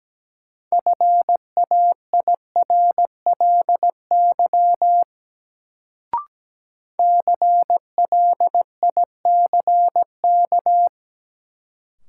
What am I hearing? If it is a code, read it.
Morse code, heard as a keyed tone, about 17 wpm, 700 Hz. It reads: FAIRLY CLICK
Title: Sentences from Top 1100 Words - New Word Drill Alt - 17wpm